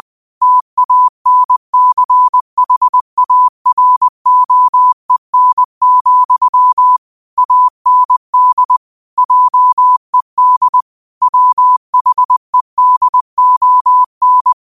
Morse code, heard as a keyed tone, about 20 wpm, 1000 Hz. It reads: TANCHAROEN, AND JED WHEDON
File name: 20wpm